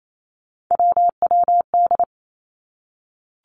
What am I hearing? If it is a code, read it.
Morse code, heard as a keyed tone, about 28 wpm, 700 Hz. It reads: WWD
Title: Three Challenging Letter-Number Combinations 28wpm